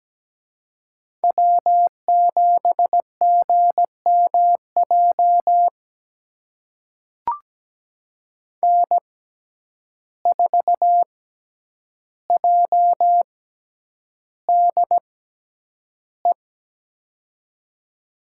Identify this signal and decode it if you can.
Morse code, heard as a keyed tone, about 17 wpm, spaced as for 9 wpm, 700 Hz. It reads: W7GMJ N4JDE
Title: USA Call Signs - 5 Characters - 6x Character Spacing - 17wpm